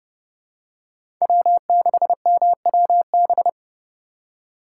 Morse code, heard as a keyed tone, about 30 wpm, 700 Hz. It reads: W6MWB
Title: USA Call Signs - 5 Characters - 2x Character Spacing - 30wpm